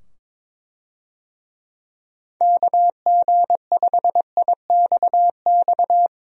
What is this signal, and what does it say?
Morse code, heard as a keyed tone, about 22 wpm, 700 Hz. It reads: KG5IXX
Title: USA Call Signs - 6 Characters - 22wpm